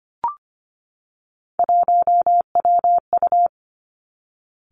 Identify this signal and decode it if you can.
Morse code, heard as a keyed tone, about 25 wpm, 700 Hz. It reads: 1WU
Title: Three Challenging Letter-Number Combinations 25wpm